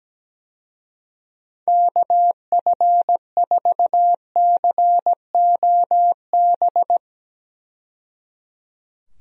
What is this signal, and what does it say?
Morse code, heard as a keyed tone, about 17 wpm, 700 Hz. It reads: KF4COB
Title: USA Call Signs - 6 Characters - 17wpm